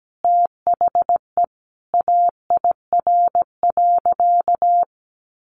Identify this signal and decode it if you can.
Morse code, heard as a keyed tone, about 17 wpm, 700 Hz. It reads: THE AIR.
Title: Easy Phrases 17wpm